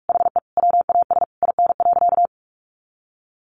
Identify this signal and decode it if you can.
Morse code, heard as a keyed tone, about 44 wpm, 700 Hz. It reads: HE WAS IN.
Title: Sentences from Top 400 Words 45wpm